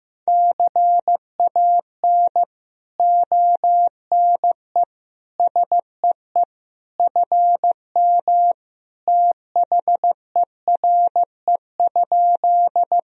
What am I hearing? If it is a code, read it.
Morse code, heard as a keyed tone, about 16 wpm, 700 Hz. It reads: CAN ONE SEE FM THERE?
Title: Sentences from Top 100 Words and Abbreviations 15wpm